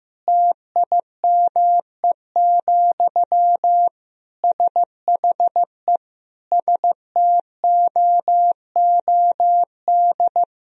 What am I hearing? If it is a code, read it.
Morse code, heard as a keyed tone, about 15 wpm, 700 Hz. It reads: TIME, SHE STOOD
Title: Sentences from Top 1000 Words - Review Part 2 - 15wpm